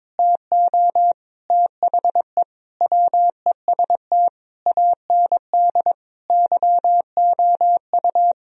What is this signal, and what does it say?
Morse code, heard as a keyed tone, about 22 wpm, 700 Hz. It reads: TO THE WEST AND YOU
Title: Sentences from Top 1000 Words - Review Part 2 - 22wpm